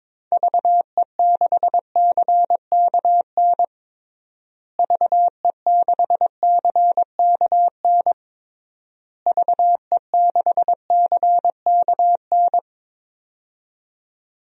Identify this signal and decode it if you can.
Morse code, heard as a keyed tone, about 22 wpm, 700 Hz. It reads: VE6CKN VE6CKN VE6CKN
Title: DXCC-100 Call Signs - 6 Characters - Repeated 3x - 22wpm